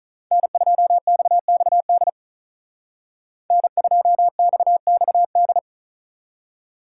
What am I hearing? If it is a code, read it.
Morse code, heard as a keyed tone, about 39 wpm, 700 Hz. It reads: N2XXD N2XXD
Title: Speed Racing - USA 1x3 Call Signs 35wpm